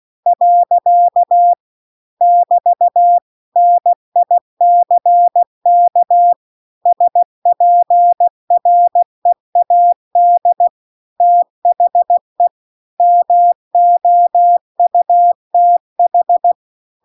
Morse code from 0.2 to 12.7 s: . <BT> NICK SPREAD THE